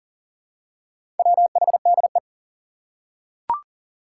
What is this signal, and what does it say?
Morse code, heard as a keyed tone, about 40 wpm, 700 Hz. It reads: WHDE
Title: Four Challenging Letter-Number Combinations 40wpm